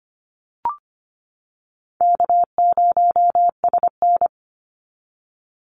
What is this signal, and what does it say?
Morse code, heard as a keyed tone, about 25 wpm, 700 Hz. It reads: K0SN
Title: Easy Call Signs - 25wpm